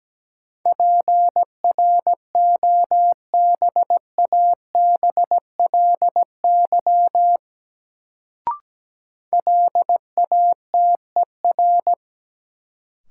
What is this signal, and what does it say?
Morse code, heard as a keyed tone, about 17 wpm, 700 Hz. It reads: PROBABLY LATER
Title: Sentences from Top 700 Words - New Word Drill - 17wpm